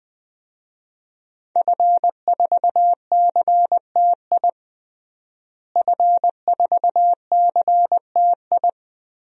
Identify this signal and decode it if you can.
Morse code, heard as a keyed tone, about 21 wpm, 700 Hz. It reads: F4CTI F4CTI
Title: DXCC-100 Call Signs - 5 Characters - Repeated 3x - 20wpm